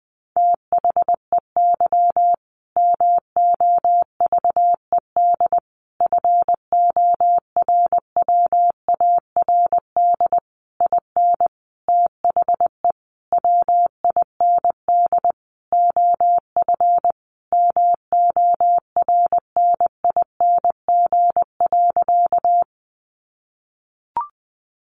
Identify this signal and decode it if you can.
Morse code, heard as a keyed tone, about 19 wpm, 700 Hz. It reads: THEY MOVED FORWARD IN THE WIND OF MORNING.
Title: A Sound Of Thunder 20wpm